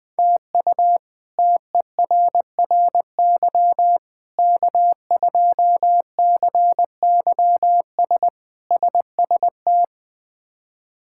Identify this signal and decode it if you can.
Morse code, heard as a keyed tone, about 20 wpm, 700 Hz. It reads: TU TERRY K2CYS SST
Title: K1USN SST QSO Practice - 20wpm